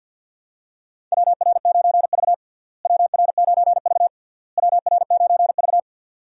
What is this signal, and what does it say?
Morse code, heard as a keyed tone, about 52 wpm, 700 Hz. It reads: WR9V WR9V WR9V
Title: USA Call Signs - 4 Characters - Repeated 3x - 50wpm